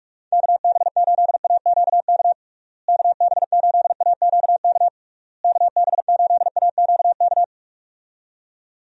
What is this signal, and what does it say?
Morse code, heard as a keyed tone, about 47 wpm, 700 Hz. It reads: KB8AQK KB8AQK KB8AQK
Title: USA Call Signs - 6 Characters - Repeated 3x - 45wpm